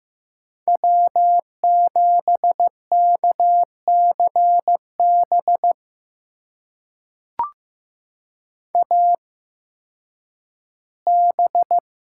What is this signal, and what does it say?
Morse code, heard as a keyed tone, about 15 wpm, spaced as for 10 wpm, 700 Hz. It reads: W7KCB AB
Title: USA Call Signs - 8x Character Spacing - 15wpm